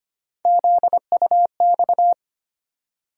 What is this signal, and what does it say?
Morse code, heard as a keyed tone, about 25 wpm, 700 Hz. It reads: ZUX
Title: USA Call Sign Postfixes - 3 Letters 25wpm